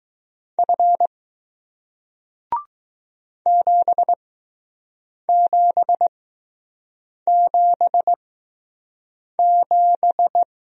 Morse code, heard as a keyed tone, about 19 wpm, spaced as for 6 wpm, 700 Hz. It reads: F 7777